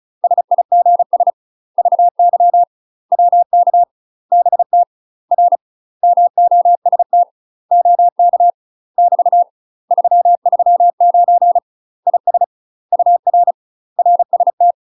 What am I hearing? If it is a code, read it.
Morse code, heard as a keyed tone, about 36 wpm, 700 Hz. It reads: SIGS VY WK BT R MOST OK <BT> 339 IS UR RST
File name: UK MASTER QSOs TRAINING / 35WPM / UK MASTER QSOs 0012